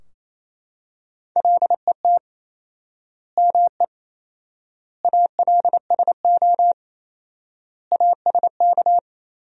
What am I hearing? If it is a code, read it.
Morse code, heard as a keyed tone, about 28 wpm, 700 Hz. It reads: LET ME ALSO ASK
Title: 4x Word-Spacing - Sentences from Top 400 Words - 28wpm